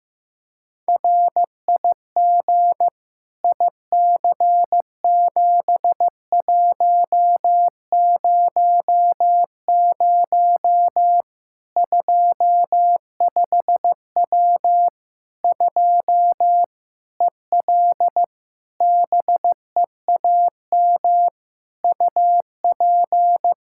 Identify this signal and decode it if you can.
Morse code, heard as a keyed tone, about 15 wpm, 700 Hz. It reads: RIG IC7100 25W 2 EL BEAM UP
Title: K1USN SST QSO Practice 15wpm